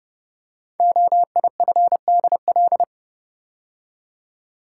Morse code, heard as a keyed tone, about 28 wpm, 690 Hz. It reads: OIFDL
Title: Code Groups - 5 Letters - 30wpm